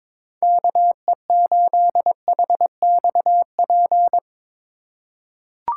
Morse code, heard as a keyed tone, about 22 wpm, 700 Hz. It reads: KE8HXP
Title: USA Call Signs - 3x Character Spacing - 22wpm